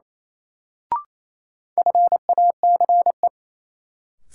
Morse code, heard as a keyed tone, about 28 wpm, 700 Hz. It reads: FACE
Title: Sets of 3 Words - New Word Drill - 28wpm